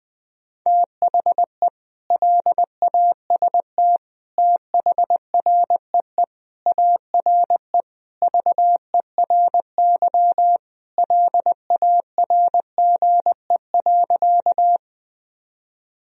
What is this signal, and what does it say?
Morse code, heard as a keyed tone, about 20 wpm, 700 Hz. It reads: THE LAST THREE ARE VERY LARGE.
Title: Sentences from Top 1000 Words - Review Part 1 - 20wpm